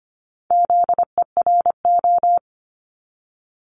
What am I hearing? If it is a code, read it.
Morse code, heard as a keyed tone, about 25 wpm, 700 Hz. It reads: ZERO
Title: Multi-letter Phonemes 25wpm